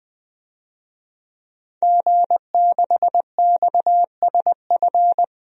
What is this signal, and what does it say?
Morse code, heard as a keyed tone, about 20 wpm, 700 Hz. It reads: G6XSF
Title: DXCC-100 Call Signs - 5 Characters - 20wpm